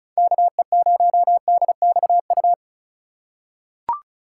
Morse code, heard as a keyed tone, about 34 wpm, 700 Hz. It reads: KE0DXU
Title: USA Call Signs - 6 Characters - 4x Character Spacing - 35wpm